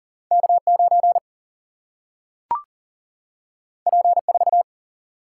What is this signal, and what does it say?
Morse code, heard as a keyed tone, about 40 wpm, 700 Hz. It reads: K9 P4
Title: Single-Letter Single-Number 40wpm